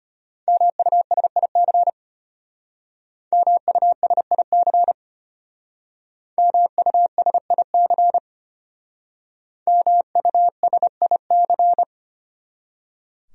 Morse code, heard as a keyed tone, about 31 wpm, 700 Hz. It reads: MUSIC MUSIC MUSIC MUSIC
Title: Speed-Racing Top 500 Words 25wpm